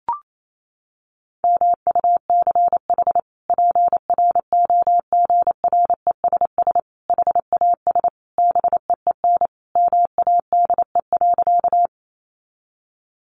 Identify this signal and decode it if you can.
Morse code, heard as a keyed tone, about 28 wpm, 700 Hz. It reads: MUCH PROGRESS HAS BEEN MADE.
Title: Sentences from Top 1400 Words 28wpm